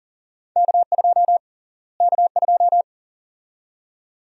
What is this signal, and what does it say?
Morse code, heard as a keyed tone, about 40 wpm, 700 Hz. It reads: K2 K2
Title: USA Call Sign Prefixes - Repeated 2x - 40wpm